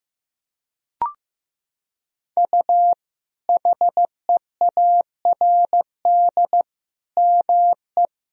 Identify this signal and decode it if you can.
Morse code, heard as a keyed tone, about 15 wpm, 700 Hz. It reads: U HEARD ME
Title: Sentences from Top 500 Words and Abbreviations 15wpm